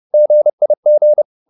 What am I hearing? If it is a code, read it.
Morse code, heard as a keyed tone, about 30 wpm, 600 Hz. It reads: GIG
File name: PR204 30